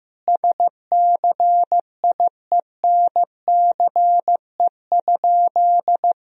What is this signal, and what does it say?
Morse code, heard as a keyed tone, about 16 wpm, 700 Hz. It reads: SCIENCE?
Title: Sentences from Top 1000 Words - Review Part 3 - 15wpm